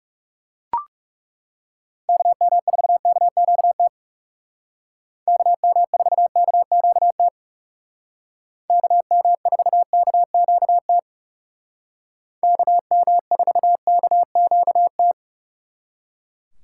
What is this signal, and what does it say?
Morse code, heard as a keyed tone, about 35 wpm, 700 Hz. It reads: KM4KQT KM4KQT KM4KQT KM4KQT
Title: Speed Racing - USA 2x3 Call Signs 30wpm